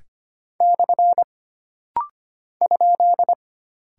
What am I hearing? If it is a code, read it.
Morse code, heard as a keyed tone, about 25 wpm, 700 Hz. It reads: / ?
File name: Single Character (Letter:Number:Punctuation:Prosign) - Rapid-Fire - 25wpm